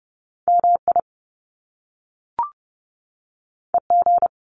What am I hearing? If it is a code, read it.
Morse code, heard as a keyed tone, about 30 wpm, 700 Hz. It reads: MI EG
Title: Top Two Letter Combinations in Words 30wpm